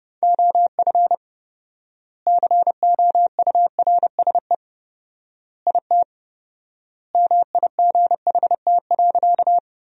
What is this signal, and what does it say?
Morse code, heard as a keyed tone, about 30 wpm, 700 Hz. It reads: OF COURSE IT MIGHT.
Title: 4x Word-Spacing - Sentences from Top 500 Words - 30wpm